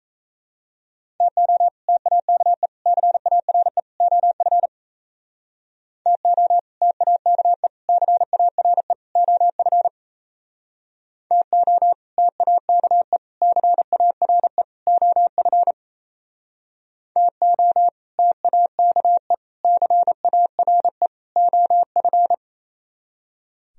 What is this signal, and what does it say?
Morse code, heard as a keyed tone, about 37 wpm, 700 Hz. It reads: TO TAKE CARE OF TO TAKE CARE OF TO TAKE CARE OF TO TAKE CARE OF